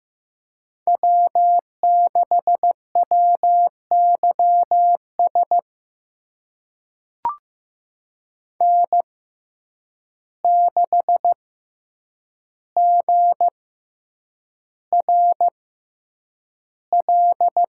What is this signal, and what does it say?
Morse code, heard as a keyed tone, about 15 wpm, spaced as for 8 wpm, 700 Hz. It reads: W6WYS N6GRL
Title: USA Call Signs - 5 Characters - 6x Character Spacing - 15wpm